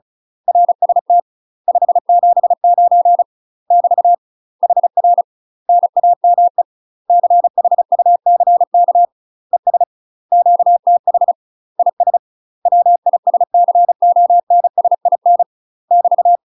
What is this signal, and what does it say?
Morse code, heard as a keyed tone, about 34 wpm, 700 Hz. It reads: RST 579 <BT> HR NAME CHUCK ES QTH IS WISCONSIN <BT>
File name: UK MASTER QSOs TRAINING / 35WPM / UK MASTER QSOs 0040